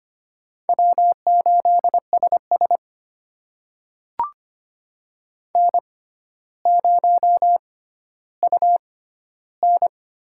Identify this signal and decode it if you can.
Morse code, heard as a keyed tone, about 25 wpm, spaced as for 13 wpm, 700 Hz. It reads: W8SS N0UN
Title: USA Call Signs - 4 Characters - 6x Character Spacing - 25wpm